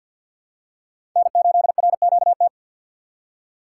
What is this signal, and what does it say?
Morse code, heard as a keyed tone, about 53 wpm, 700 Hz. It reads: N8RQT
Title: USA Call Signs - 5 Characters - 3x Character Spacing - 50wpm